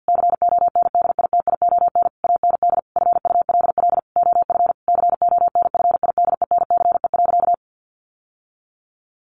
Morse code, heard as a keyed tone, about 50 wpm, 700 Hz. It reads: CONDITION AND FULL OF CONFIDENCE.
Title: Sentences from Top 1500 Words 50wpm